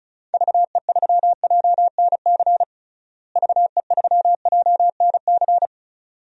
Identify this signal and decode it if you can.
Morse code, heard as a keyed tone, about 34 wpm, 700 Hz. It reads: VE3JNC VE3JNC
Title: DXCC-100 Call Signs - 6 Characters - Repeated 2x - 35wpm